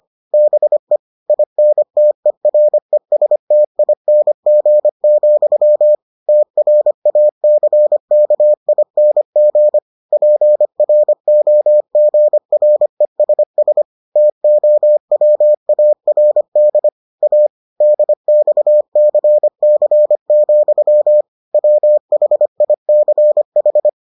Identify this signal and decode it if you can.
Morse code, heard as a keyed tone, about 25 wpm, 600 Hz. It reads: BE INTERESTING, TRACKING PROGRESS TOWARD A DXCC, WHICH